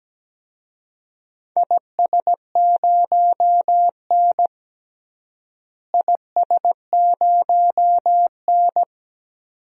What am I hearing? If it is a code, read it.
Morse code, heard as a keyed tone, about 17 wpm, 700 Hz. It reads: IS0N IS0N
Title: DXCC-100 Call Signs - 4 Characters - Repeated 2x - 17wpm